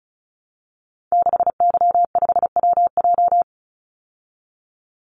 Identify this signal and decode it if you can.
Morse code, heard as a keyed tone, about 34 wpm, 700 Hz. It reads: 6Y5WJ
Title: Call Signs Repeated 3x - 35wpm